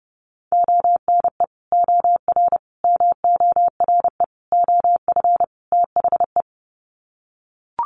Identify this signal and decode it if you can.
Morse code, heard as a keyed tone, about 30 wpm, 700 Hz. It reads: ONE OR MORE OF THE
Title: Sets of 5 Words Encore 30wpm